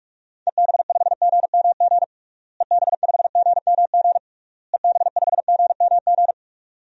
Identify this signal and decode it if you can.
Morse code, heard as a keyed tone, about 46 wpm, 700 Hz. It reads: EB5GMG EB5GMG EB5GMG
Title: DXCC-100 Call Signs - 6 Characters - Repeated 3x - 45wpm